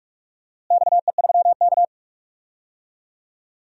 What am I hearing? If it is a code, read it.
Morse code, heard as a keyed tone, about 45 wpm, 700 Hz. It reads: XE3K
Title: DXCC-100 Call Signs - 4 Characters - Repeated 2x - 45wpm